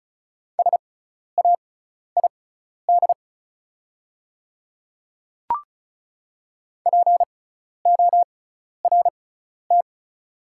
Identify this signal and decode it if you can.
Morse code, heard as a keyed tone, about 36 wpm, spaced as for 12 wpm, 700 Hz. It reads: SAID PORT
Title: 6x Character Spacing - Top 300 Words - 35wpm